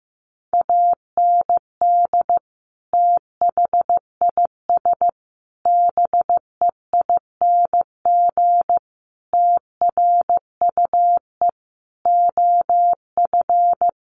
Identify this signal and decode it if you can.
Morse code, heard as a keyed tone, about 14 wpm, 700 Hz. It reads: AND THIS BEING TRUE OF
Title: Sentences from Top 1400 Words 15wpm